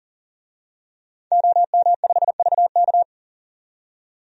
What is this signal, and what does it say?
Morse code, heard as a keyed tone, about 40 wpm, 700 Hz. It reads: OM5VK
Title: DXCC-100 Call Signs - 5 Characters - 40wpm